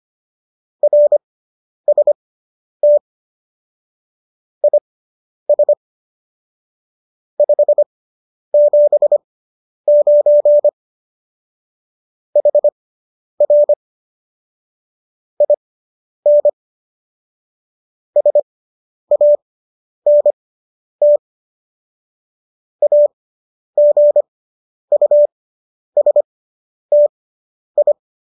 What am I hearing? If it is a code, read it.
Morse code, heard as a keyed tone, about 25 wpm, spaced as for 10 wpm, 590 Hz. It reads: RST IS 579 HR IN SANT AGUSTI